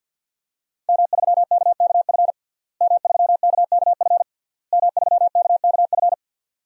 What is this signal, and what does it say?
Morse code, heard as a keyed tone, about 49 wpm, 700 Hz. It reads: M3KKF M3KKF M3KKF
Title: DXCC-100 Call Signs - 5 Characters - Repeated 3x - 50wpm